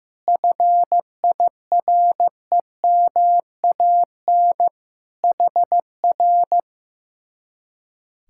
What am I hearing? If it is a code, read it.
Morse code, heard as a keyed tone, about 15 wpm, 700 Hz. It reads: FIREMAN HR
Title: Speed-Racing Common QSO Phrases Encore - 15wpm